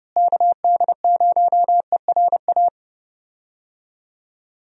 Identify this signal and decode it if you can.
Morse code, heard as a keyed tone, about 30 wpm, 700 Hz. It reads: KD0ERA
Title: USA Call Signs - 6 Characters - Repeated 2x - 30wpm